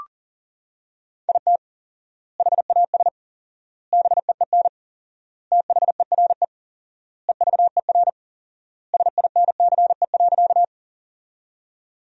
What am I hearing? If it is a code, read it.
Morse code, heard as a keyed tone, about 40 wpm, 700 Hz. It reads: IT HAS BEEN THERE EVER SINCE.
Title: 4x Word-Spacing - Sentences from Top 400 Words - 40wpm